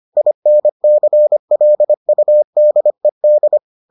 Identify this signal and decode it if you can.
Morse code, heard as a keyed tone, about 25 wpm, 600 Hz. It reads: INCLUDED